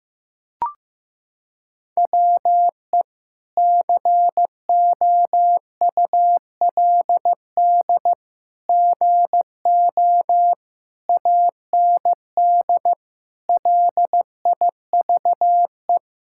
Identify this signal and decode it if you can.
Morse code, heard as a keyed tone, about 15 wpm, 700 Hz. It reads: WE COULD GO AND LIVE
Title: Sentences from Top 1600 Words 15wpm